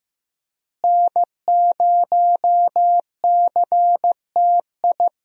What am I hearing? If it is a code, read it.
Morse code, heard as a keyed tone, about 15 wpm, 700 Hz. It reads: N0CTI
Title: USA Call Signs - 5 Characters - Repeated 3x - 15wpm